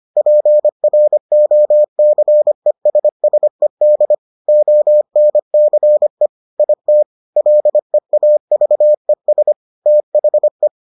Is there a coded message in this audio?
Yes: Morse code from 0.2 to 10.7 s, heard as a keyed tone, about 25 wpm, 600 Hz. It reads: PROCESSED ONCE IT LEAVES THE